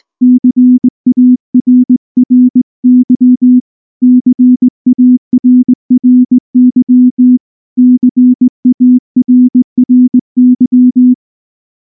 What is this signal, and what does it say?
Morse code, heard as a keyed tone, about 18 wpm, 250 Hz. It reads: CARRY CARRY CARRY
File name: morse / morse-training-20WPM-10